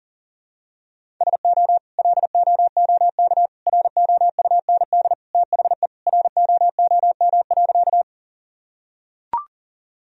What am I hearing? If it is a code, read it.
Morse code, heard as a keyed tone, about 37 wpm, 700 Hz. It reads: SO LOOK ROUND THE ROOM.
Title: Sentences from Top 1000 Words - Review Part 1 - 40wpm